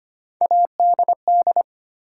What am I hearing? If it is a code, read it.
Morse code, heard as a keyed tone, about 25 wpm, 700 Hz. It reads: ADD
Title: Sentences from Top 1000 Words - Review Part 1 - New Word Drill - 25wpm